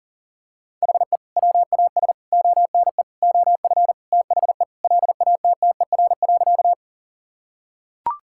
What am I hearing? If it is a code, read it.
Morse code, heard as a keyed tone, about 40 wpm, 700 Hz. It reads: HE WAS ONE OF THE LATTER.
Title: Sentences from Top 1600 Words 40wpm